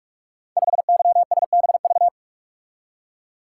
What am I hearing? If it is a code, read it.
Morse code, heard as a keyed tone, about 44 wpm, 700 Hz. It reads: 5YSBV
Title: Code Groups - 5 Letter:Numbers - 45wpm